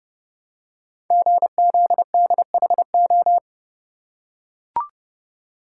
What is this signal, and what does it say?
Morse code, heard as a keyed tone, about 30 wpm, 700 Hz. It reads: GZDHO
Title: Code Groups - 5 Letters - 30wpm